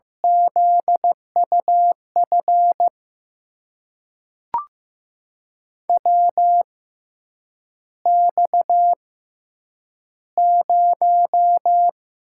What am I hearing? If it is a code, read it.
Morse code, heard as a keyed tone, about 16 wpm, spaced as for 8 wpm, 700 Hz. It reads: ZUF WX0
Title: USA Call Signs - 6 Characters - 6x Character Spacing - 15wpm